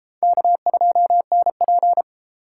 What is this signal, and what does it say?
Morse code, heard as a keyed tone, about 30 wpm, 700 Hz. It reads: K2NP